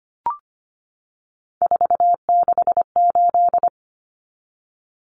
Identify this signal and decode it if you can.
Morse code, heard as a keyed tone, about 25 wpm, 690 Hz. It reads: 468
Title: Numbers - 3-Digit 25wpm